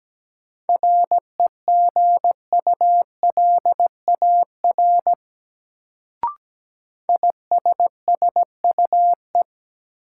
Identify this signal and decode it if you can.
Morse code, heard as a keyed tone, about 17 wpm, 700 Hz. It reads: REGULAR ISSUE
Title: Sentences from Top 900 Words - New Word Drill - 17wpm